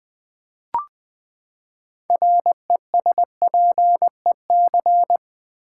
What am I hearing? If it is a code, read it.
Morse code, heard as a keyed tone, about 20 wpm, 700 Hz. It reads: RESPEC
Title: Build A Word - 6 Letters At A Time - 12-Letter Words - 20wpm